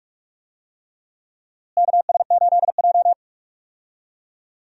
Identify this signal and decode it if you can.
Morse code, heard as a keyed tone, about 47 wpm, 700 Hz. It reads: KS8J
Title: USA Call Signs - 4 Characters - Repeated 3x - 45wpm